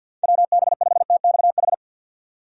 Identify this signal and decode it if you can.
Morse code, heard as a keyed tone, about 52 wpm, 700 Hz. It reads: WB5TXH
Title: USA Call Signs - 6 Characters - 6x Character Spacing - 50wpm